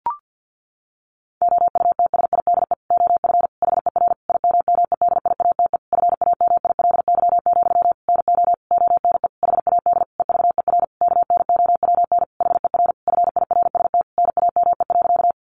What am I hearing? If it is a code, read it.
Morse code, heard as a keyed tone, about 50 wpm, 700 Hz. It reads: OUTSIDE OF HER IMMEDIATE FAMILY, NO ONE HAD EVER KNOWN HER FIRST NAME.
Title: Sentences from Top 1200 Words 50wpm